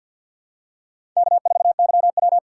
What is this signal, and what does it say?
Morse code, heard as a keyed tone, about 47 wpm, 700 Hz. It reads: K4YW
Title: USA Call Signs - 4 Characters - 6x Character Spacing - 50wpm